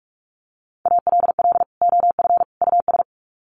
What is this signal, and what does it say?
Morse code, heard as a keyed tone, about 44 wpm, 700 Hz. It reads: ALL OF US
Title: Sets of 3 Words 45wpm